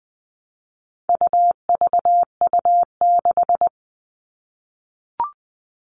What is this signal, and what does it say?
Morse code, heard as a keyed tone, about 20 wpm, 700 Hz. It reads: UVU6
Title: All Cut Numbers - 4-Digit 20wpm